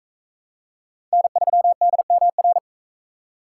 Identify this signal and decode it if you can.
Morse code, heard as a keyed tone, about 38 wpm, 700 Hz. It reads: N3DMR